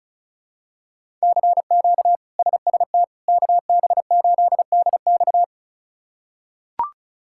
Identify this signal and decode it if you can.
Morse code, heard as a keyed tone, about 36 wpm, 700 Hz. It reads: CQ SST KB8DX
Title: K1USN SST QSO Practice - 35wpm